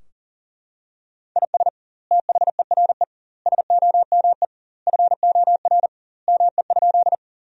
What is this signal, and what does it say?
Morse code, heard as a keyed tone, about 40 wpm, 700 Hz. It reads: IS THERE SOME FOR ME?
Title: 2x Word-Spacing - Sentences from Top 200 Words - 40wpm